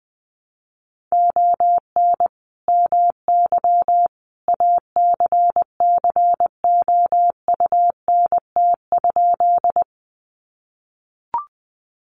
Morse code, heard as a keyed tone, about 20 wpm, 700 Hz. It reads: ON MY ACCOUNT?
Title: Sentences from Top 900 Words 20wpm